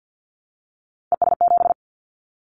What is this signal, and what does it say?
Morse code, heard as a keyed tone, about 50 wpm, 700 Hz. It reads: ES7
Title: Callsign Prefixes for Top 50 DXCC - 50wpm